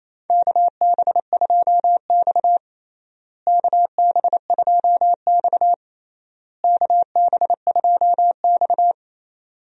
Morse code, heard as a keyed tone, about 27 wpm, 700 Hz. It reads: KB2X KB2X KB2X
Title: USA Call Signs - 4 Characters - Repeated 3x - 28wpm